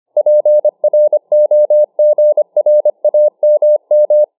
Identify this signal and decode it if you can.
Morse code, heard as a keyed tone, about 25 wpm, 600 Hz. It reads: PROGRAMM